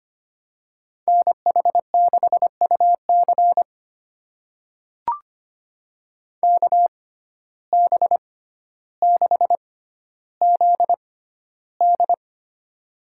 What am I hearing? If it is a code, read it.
Morse code, heard as a keyed tone, about 25 wpm, spaced as for 13 wpm, 700 Hz. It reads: NH6UC KB6ZD
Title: USA Call Signs - 5 Characters - 6x Character Spacing - 25wpm